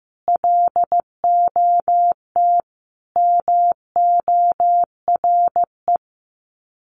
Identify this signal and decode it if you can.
Morse code, heard as a keyed tone, about 15 wpm, 700 Hz. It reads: LOT MORE
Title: Sets of 3 Words 15wpm